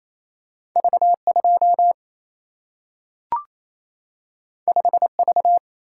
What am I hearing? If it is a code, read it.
Morse code, heard as a keyed tone, about 28 wpm, 700 Hz. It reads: V 2 5 V
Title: Single-Letter Single-Number 28wpm